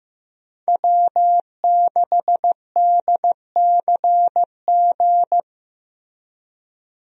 Morse code, heard as a keyed tone, about 15 wpm, 700 Hz. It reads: W6DCG
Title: USA Call Signs - 5 Characters - Repeated 2x - 15wpm